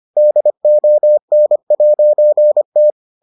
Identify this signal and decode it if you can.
Morse code, heard as a keyed tone, about 25 wpm, 600 Hz. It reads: DON'T